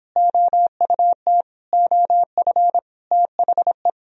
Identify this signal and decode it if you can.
Morse code, heard as a keyed tone, about 31 wpm, 700 Hz. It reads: OUT OF THE